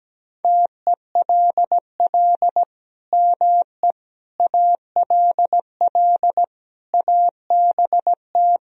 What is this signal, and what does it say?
Morse code, heard as a keyed tone, about 17 wpm, 700 Hz. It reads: TELL ME ALL ABT
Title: Sentences from Top 200 Words and Abbreviations 17wpm